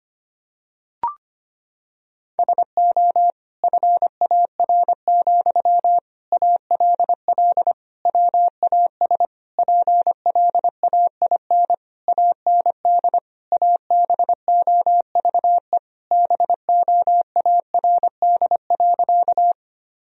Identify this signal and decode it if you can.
Morse code, heard as a keyed tone, about 25 wpm, 700 Hz. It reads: SO FAR, ALL WAS PLAIN AND ABOVE BOARD.
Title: Sentences from Top 1000 Words - Review Part 4 - 25wpm